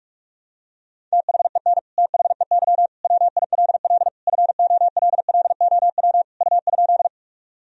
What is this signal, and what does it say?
Morse code, heard as a keyed tone, about 46 wpm, 700 Hz. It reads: THEN THEY WILL FOLLOW U?
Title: Sentences from Top 200 Words and Abbreviations 45wpm